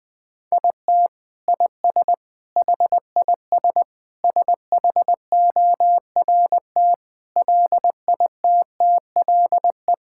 Morse code, heard as a keyed tone, about 21 wpm, 700 Hz. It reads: IT IS HIS SHORT LITTLE
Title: Sentences from Top 1600 Words 20wpm